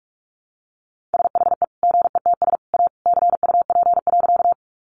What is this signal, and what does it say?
Morse code, heard as a keyed tone, about 44 wpm, 700 Hz. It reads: SHE GETS A CUP.
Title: Easy Phrases 45wpm